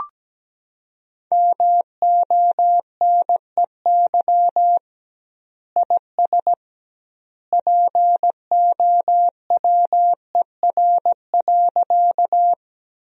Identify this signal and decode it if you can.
Morse code, heard as a keyed tone, about 17 wpm, 700 Hz. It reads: MONEY IS POWER.
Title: 2x Word-Spacing - Sentences from Top 500 Words - 17wpm